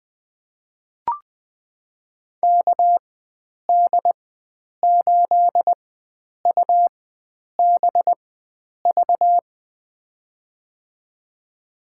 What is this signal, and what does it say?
Morse code, heard as a keyed tone, about 20 wpm, spaced as for 9 wpm, 700 Hz. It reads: KD8UBV
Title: USA Call Signs - 6 Characters - 4x Character Spacing - 20wpm